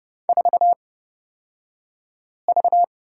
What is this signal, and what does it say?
Morse code, heard as a keyed tone, about 30 wpm, spaced as for 5 wpm, 700 Hz. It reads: 4V
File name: Lesson 37d - Letter x - Two Characters 30wpm